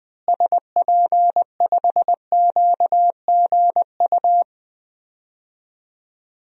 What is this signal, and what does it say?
Morse code, heard as a keyed tone, about 20 wpm, 700 Hz. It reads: SP5QGU
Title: DXCC-100 Call Signs - 6 Characters - Repeated 3x - 20wpm